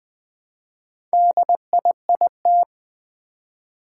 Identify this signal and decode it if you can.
Morse code, heard as a keyed tone, about 20 wpm, 700 Hz. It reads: DIIT
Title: Four Challenging Letter-Number Combinations 20wpm